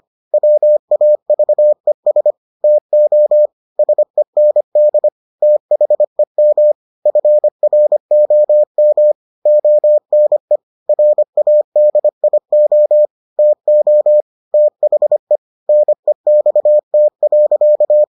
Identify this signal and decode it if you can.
Morse code, heard as a keyed tone, about 25 wpm, 600 Hz. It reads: WAVES TO SEND THEM FROM ONE RADIO TO THE NEXT.